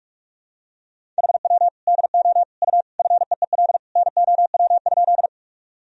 Morse code, heard as a keyed tone, about 42 wpm, 700 Hz. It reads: HW DO U FEEL NOW?
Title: Sentences from Top 400 Words and Abbreviations 45wpm